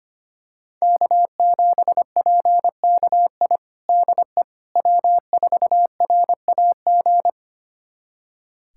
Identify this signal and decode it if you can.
Morse code, heard as a keyed tone, about 25 wpm, 700 Hz. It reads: K7PKI DE W4RAG
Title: USA Callsign DE Callsign - 5 Characters - 25wpm